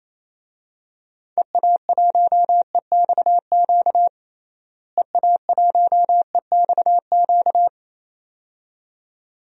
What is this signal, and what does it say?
Morse code, heard as a keyed tone, about 27 wpm, 700 Hz. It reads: EA1EXQ EA1EXQ
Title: DXCC-100 Call Signs - 6 Characters - Repeated 2x - 28wpm